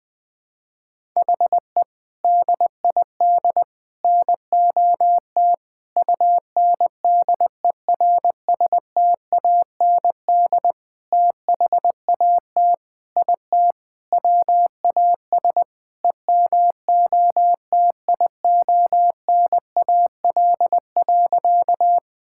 Morse code, heard as a keyed tone, about 21 wpm, 700 Hz. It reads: HE DID NOT UNDERSTAND THAT IT WAS EMOTIONAL.
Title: Sentences from Top 1000 Words - Review Part 5 - 20wpm